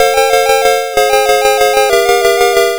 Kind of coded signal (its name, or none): none